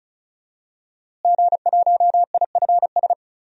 Morse code, heard as a keyed tone, about 34 wpm, 700 Hz. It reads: G1IFS
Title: DXCC-100 Call Signs - 5 Characters - Repeated 3x - 35wpm